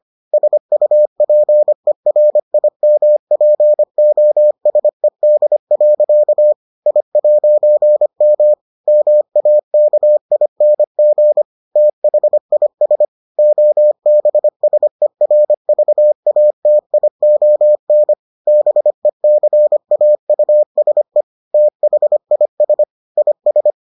Morse code, heard as a keyed tone, about 25 wpm, 600 Hz. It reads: SUPERIMPOSED. I'M MAKING THIS OBSERVATION BECAUSE THIS IS